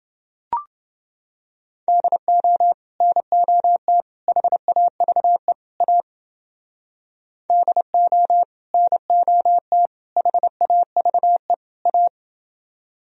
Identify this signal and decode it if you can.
Morse code, heard as a keyed tone, about 30 wpm, 700 Hz. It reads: DO NOT HAVE A DO NOT HAVE A